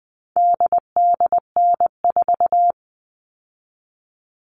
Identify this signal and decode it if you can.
Morse code, heard as a keyed tone, about 20 wpm, 700 Hz. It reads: DDN4
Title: All Cut Numbers - 4-Digit 20wpm